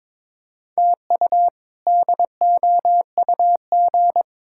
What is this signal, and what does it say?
Morse code, heard as a keyed tone, about 22 wpm, 700 Hz. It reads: TU DOUG
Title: K1USN SST QSO Practice - 22wpm